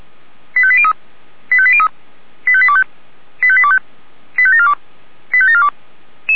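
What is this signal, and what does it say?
Five-tone selective call (ZVEI), heard as quick runs of five tones, tones of about 70 ms. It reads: 85803 85803 85825 85825 85742 85742
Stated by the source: ZVEI five-tone call received on SDR.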